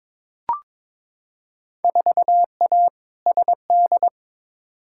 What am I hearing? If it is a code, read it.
Morse code, heard as a keyed tone, about 22 wpm, 700 Hz. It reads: 4A SD
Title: Field Day Exchanges - 22wpm